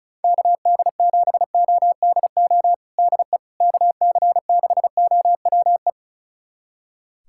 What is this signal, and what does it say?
Morse code, heard as a keyed tone, about 36 wpm, 690 Hz. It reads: KD7ODO DE KC6OWE
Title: USA Callsign DE Callsign - 6 Characters - 35wpm